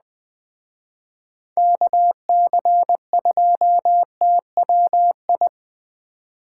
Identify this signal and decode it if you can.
Morse code, heard as a keyed tone, about 20 wpm, 700 Hz. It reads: KC2TWI
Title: Speed Racing - USA Call Signs 17wpm